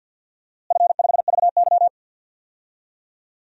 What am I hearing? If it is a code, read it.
Morse code, heard as a keyed tone, about 48 wpm, 700 Hz. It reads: F5VY
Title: DXCC-100 Call Signs - 4 Characters - Repeated 3x - 50wpm